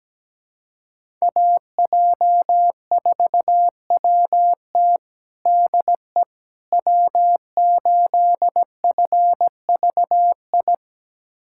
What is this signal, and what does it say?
Morse code, heard as a keyed tone, about 17 wpm, 700 Hz. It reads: AJ4WT DE W8FVI
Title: USA Callsign DE Callsign - 5 Characters - 17wpm